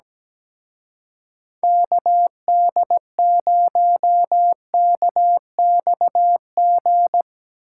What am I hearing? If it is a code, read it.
Morse code, heard as a keyed tone, about 18 wpm, 700 Hz. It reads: KD0KXG
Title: Speed Racing - USA Call Signs 17wpm